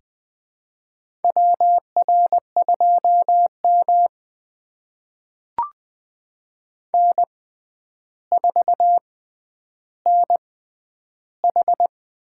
Morse code, heard as a keyed tone, about 20 wpm, spaced as for 10 wpm, 700 Hz. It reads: WR2M N4NH
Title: USA Call Signs - 6x Character Spacing - 20wpm